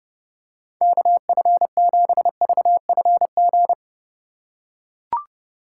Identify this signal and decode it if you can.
Morse code, heard as a keyed tone, about 30 wpm, 700 Hz. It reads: KF7VFG